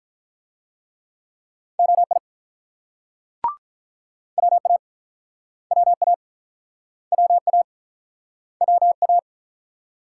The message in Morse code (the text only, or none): OI WA WA WA WA